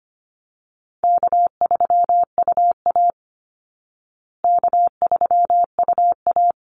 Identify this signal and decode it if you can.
Morse code, heard as a keyed tone, about 25 wpm, 700 Hz. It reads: K3UA K3UA
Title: Easy Call Signs Repeated 2x - 25wpm